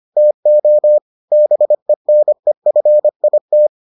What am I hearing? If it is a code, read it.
Morse code, heard as a keyed tone, about 25 wpm, 600 Hz. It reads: TO BENEFIT